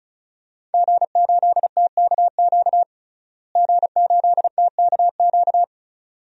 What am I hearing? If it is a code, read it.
Morse code, heard as a keyed tone, about 34 wpm, 700 Hz. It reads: G8TKQ G8TKQ
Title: DXCC-100 Call Signs - 5 Characters - Repeated 2x - 35wpm